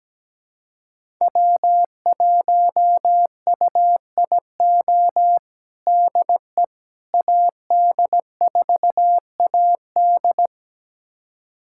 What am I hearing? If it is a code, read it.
Morse code, heard as a keyed tone, about 17 wpm, 700 Hz. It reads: W1UIO DE AD4AD
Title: USA Callsign DE Callsign - 5 Characters - 17wpm